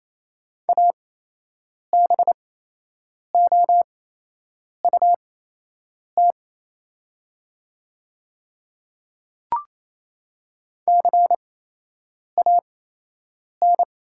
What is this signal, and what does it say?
Morse code, heard as a keyed tone, about 28 wpm, spaced as for 8 wpm, 700 Hz. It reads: ABOUT CAN
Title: 8x Character Spacing - Top 100 Words - 28wpm